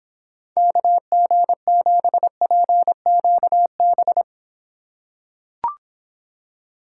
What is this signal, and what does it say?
Morse code, heard as a keyed tone, about 25 wpm, 700 Hz. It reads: KG7PQB